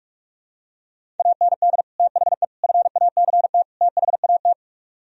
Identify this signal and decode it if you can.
Morse code, heard as a keyed tone, about 48 wpm, 700 Hz. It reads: AND THE FACT THAT